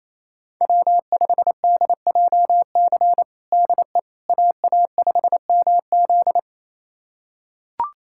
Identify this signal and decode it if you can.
Morse code, heard as a keyed tone, about 27 wpm, 700 Hz. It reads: W5DJC DE AA5MZ
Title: USA Callsign DE Callsign - 5 Characters - 28wpm